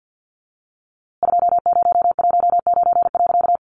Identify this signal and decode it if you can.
Morse code, heard as a keyed tone, about 50 wpm, 700 Hz. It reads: 2019.
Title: A Sound Of Thunder 50wpm